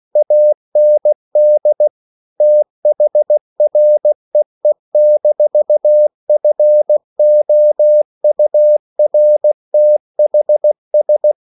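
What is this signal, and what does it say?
Morse code, heard as a keyed tone, about 16 wpm, 600 Hz. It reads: AND THREE-FOURTHS
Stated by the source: English quotations sent in CW as practice at 16 words per minute.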